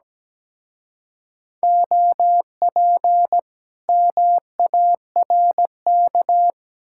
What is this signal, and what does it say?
Morse code, heard as a keyed tone, about 20 wpm, 700 Hz. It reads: OP MARK
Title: Speed-Racing Common QSO Phrases - 17wpm